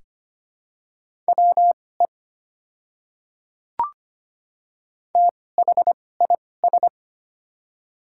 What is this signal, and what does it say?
Morse code, heard as a keyed tone, about 25 wpm, spaced as for 18 wpm, 700 Hz. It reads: WE THIS
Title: 2x Character Spacing - Top 100 Words - 25wpm